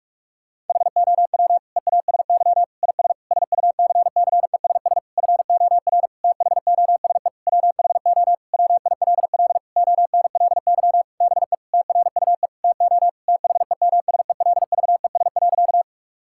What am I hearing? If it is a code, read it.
Morse code, heard as a keyed tone, about 46 wpm, 700 Hz. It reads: HOW EASY IS SUCCESS FOR THOSE WHO WILL ONLY BE TRUE TO THEMSELVES.
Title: Sentences from Top 1000 Words - Review Part 6 - 45wpm